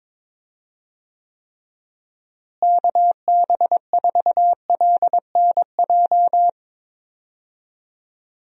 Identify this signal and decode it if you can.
Morse code, heard as a keyed tone, about 22 wpm, 700 Hz. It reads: KB4LNJ
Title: USA Call Signs - 6 Characters - Repeated 3x - 22wpm